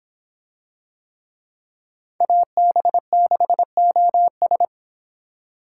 Morse code, heard as a keyed tone, about 24 wpm, 700 Hz. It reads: AB6OS